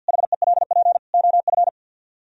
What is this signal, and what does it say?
Morse code, heard as a keyed tone, about 66 wpm, spaced as for 58 wpm, 700 Hz. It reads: HELP OF
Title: Speed-Racing - Sets of 4 Words 50wpm